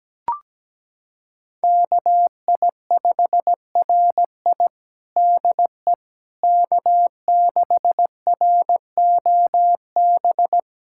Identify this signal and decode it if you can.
Morse code, heard as a keyed tone, about 17 wpm, 700 Hz. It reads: KI5RI DE K6ROB
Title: USA Callsign DE Callsign - 5 Characters - 17wpm